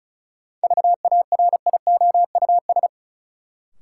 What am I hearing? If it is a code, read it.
Morse code, heard as a keyed tone, about 34 wpm, 700 Hz. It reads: VARIOUS
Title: Sentences from Top 1000 Words - Review Part 4 - New Word Drill - 35wpm